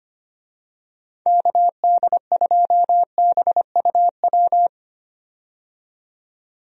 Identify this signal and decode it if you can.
Morse code, heard as a keyed tone, about 25 wpm, 700 Hz. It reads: KD2BUW
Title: USA Call Signs - 6 Characters - 25wpm